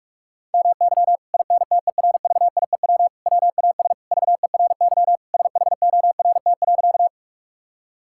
Morse code, heard as a keyed tone, about 41 wpm, 700 Hz. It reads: MY INTERVIEW WAS VERY SHORT.
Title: Sentences from Top 1600 Words 45wpm